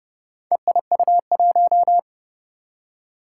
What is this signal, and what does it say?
Morse code, heard as a keyed tone, about 32 wpm, 700 Hz. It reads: EIU1
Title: Four Challenging Letter-Number Combinations 30wpm